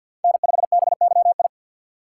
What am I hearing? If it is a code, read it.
Morse code, heard as a keyed tone, about 53 wpm, 700 Hz. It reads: N5BYI
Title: USA Call Signs - 4x Character Spacing - 50wpm